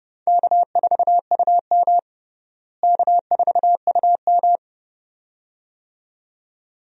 Morse code, heard as a keyed tone, about 30 wpm, 700 Hz. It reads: K4UM K4UM
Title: USA Call Signs - 4 Characters - Repeated 2x - 30wpm